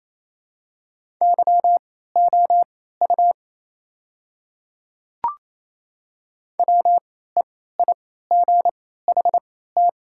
Morse code, heard as a keyed tone, about 28 wpm, spaced as for 16 wpm, 700 Hz. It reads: YOU WEIGHT
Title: 3x Character Spacing - Top 500 Words - 28wpm